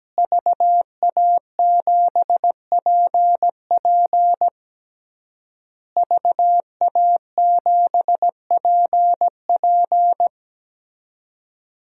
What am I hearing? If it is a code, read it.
Morse code, heard as a keyed tone, about 17 wpm, 700 Hz. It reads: VA7PP VA7PP
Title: DXCC-100 Call Signs - 5 Characters - Repeated 3x - 17wpm